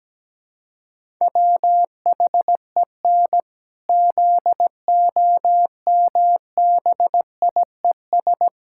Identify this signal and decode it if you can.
Morse code, heard as a keyed tone, about 17 wpm, 700 Hz. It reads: WHEN ZOMBIES
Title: Pangram Challenge v2 - 17wpm